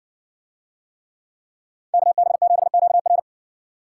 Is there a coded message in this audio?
Yes: Morse code from 1.9 to 3.2 s, heard as a keyed tone, about 60 wpm, 700 Hz. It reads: KB7QR